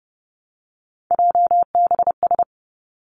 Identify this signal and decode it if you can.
Morse code, heard as a keyed tone, about 31 wpm, 700 Hz. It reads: JBS
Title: Three Challenging Letter-Number Combinations 30wpm